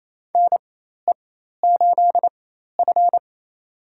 Morse code, heard as a keyed tone, about 28 wpm, spaced as for 13 wpm, 700 Hz. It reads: NE8F
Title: USA Call Signs - 4 Characters - 4x Character Spacing - 28wpm